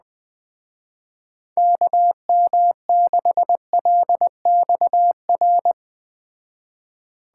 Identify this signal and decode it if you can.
Morse code, heard as a keyed tone, about 20 wpm, 700 Hz. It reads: KM6LXR